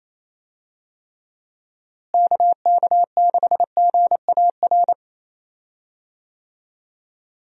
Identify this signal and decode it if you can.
Morse code, heard as a keyed tone, about 28 wpm, 700 Hz. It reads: KK6GAR
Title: USA Call Signs - 6 Characters - Repeated 2x - 28wpm